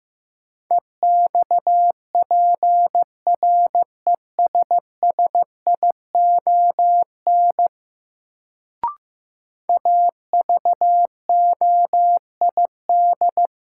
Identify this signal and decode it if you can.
Morse code, heard as a keyed tone, about 15 wpm, 700 Hz. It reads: EXPRESSION AVOID
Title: Sentences from Top 1200 Words - New Word Drill Alt - 15wpm